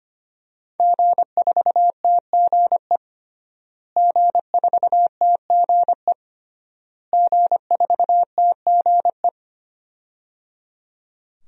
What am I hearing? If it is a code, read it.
Morse code, heard as a keyed tone, about 25 wpm, 700 Hz. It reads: G4TGE G4TGE G4TGE
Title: DXCC-100 Call Signs - 5 Characters - Repeated 3x - 25wpm